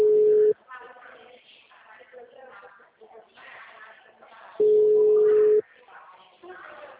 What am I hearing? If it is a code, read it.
Ringback tone (Europe, 425 Hz).